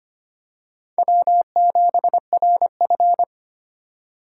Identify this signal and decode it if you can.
Morse code, heard as a keyed tone, about 25 wpm, 700 Hz. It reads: W7RF
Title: USA Call Signs - 3x Character Spacing - 25wpm